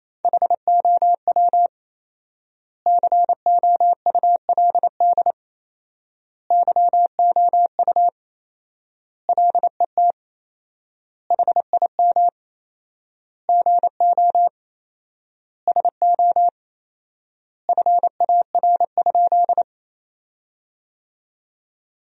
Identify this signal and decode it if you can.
Morse code, heard as a keyed tone, about 28 wpm, 700 Hz. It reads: HOW COULD YOU LET HIM GO SO FAR?
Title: 4x Word-Spacing - Sentences from Top 300 Words - 28wpm